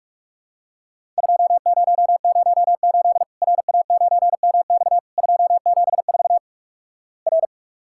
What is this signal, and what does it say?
Morse code, heard as a keyed tone, about 47 wpm, 700 Hz. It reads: 2008 RA9MX 274 R
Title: Worked All Europe DX Contest - QTC Exchange - 45wpm